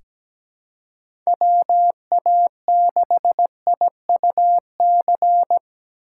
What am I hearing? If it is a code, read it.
Morse code, heard as a keyed tone, about 17 wpm, 700 Hz. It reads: WA6IUC
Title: USA Call Signs - 2x Character Spacing - 17wpm